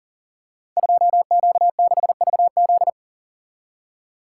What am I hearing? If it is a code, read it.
Morse code, heard as a keyed tone, about 40 wpm, 700 Hz. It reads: 2Q6VZ
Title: Code Groups - 5 Letter:Numbers - 40wpm